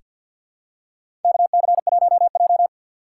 Morse code, heard as a keyed tone, about 48 wpm, 700 Hz. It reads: KC1J
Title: USA Call Signs - 4 Characters - 3x Character Spacing - 50wpm